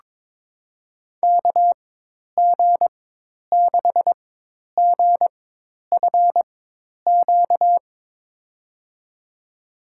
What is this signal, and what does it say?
Morse code, heard as a keyed tone, about 22 wpm, spaced as for 10 wpm, 700 Hz. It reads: KG6GFQ
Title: USA Call Signs - 6 Characters - 4x Character Spacing - 22wpm